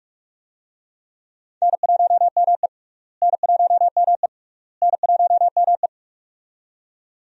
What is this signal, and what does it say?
Morse code, heard as a keyed tone, about 41 wpm, 700 Hz. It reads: N1ME N1ME N1ME
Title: USA Call Signs - 4 Characters - Repeated 3x - 45wpm